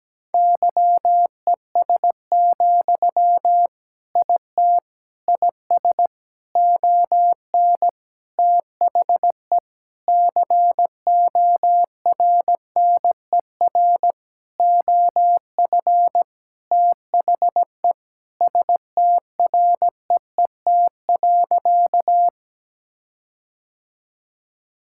Morse code, heard as a keyed tone, about 17 wpm, 700 Hz. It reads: YES, IT IS ON THE CORNER OF THE STREET.
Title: Sentences from Top 1600 Words 17wpm